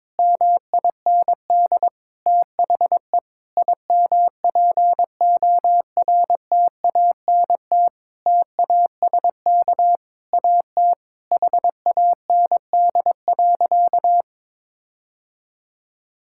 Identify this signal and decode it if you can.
Morse code, heard as a keyed tone, about 22 wpm, 700 Hz. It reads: MIND THE IMPORTANT TASK AT HAND.
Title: Sentences from Top 1600 Words 22wpm